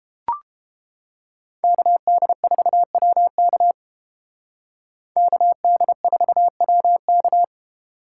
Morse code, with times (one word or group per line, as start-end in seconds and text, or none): 1.6-3.7 KD4WK
5.2-7.4 KD4WK